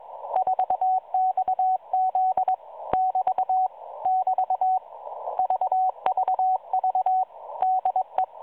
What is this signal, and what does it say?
Morse code, heard as a keyed tone, about 22 wpm, 750 Hz, with radio noise. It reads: 4XZ <BT> <BT> VVV DE
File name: 4XZ Shazepe